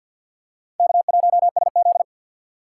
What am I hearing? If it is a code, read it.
Morse code, heard as a keyed tone, about 47 wpm, 700 Hz. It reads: K1SZ
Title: USA Call Signs - 6x Character Spacing - 50wpm